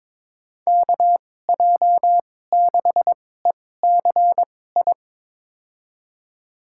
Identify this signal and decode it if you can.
Morse code, heard as a keyed tone, about 22 wpm, spaced as for 16 wpm, 700 Hz. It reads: KJ6ECI
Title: USA Call Signs - 6 Characters - 2x Character Spacing - 22wpm